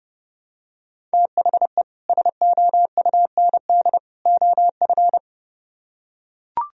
Morse code, heard as a keyed tone, about 30 wpm, 700 Hz. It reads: THE SOUND OF